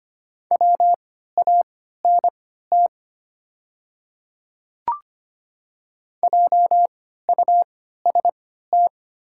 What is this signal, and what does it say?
Morse code, heard as a keyed tone, about 25 wpm, spaced as for 14 wpm, 690 Hz. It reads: WANT JUST
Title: 3x Character Spacing - Top 300 Words - 25wpm